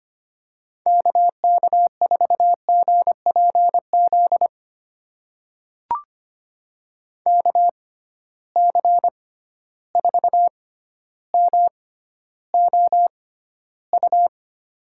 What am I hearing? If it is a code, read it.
Morse code, heard as a keyed tone, about 25 wpm, spaced as for 13 wpm, 700 Hz. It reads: KK4GPZ KC4MOU
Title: USA Call Signs - 6 Characters - 6x Character Spacing - 25wpm